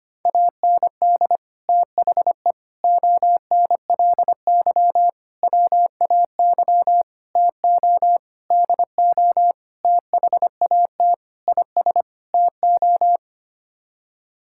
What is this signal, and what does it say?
Morse code, heard as a keyed tone, about 25 wpm, 700 Hz. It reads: AND THE ONLY WAY TO DO THAT IS TO
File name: Sets of 9 Words - 25wpm